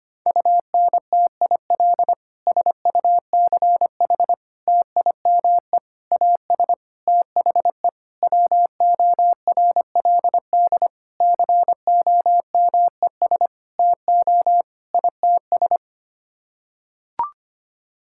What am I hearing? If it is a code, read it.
Morse code, heard as a keyed tone, about 25 wpm, 700 Hz. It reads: UNTIL SUCH TIME AS THE WORLD COMES TO ITS
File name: Sets of 9 Words - 25wpm